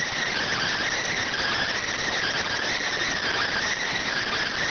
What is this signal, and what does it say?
Emergency Alert System (SAME) tones, heard as harsh buzzing bursts. It is heard from 1.2 to 3.7 s.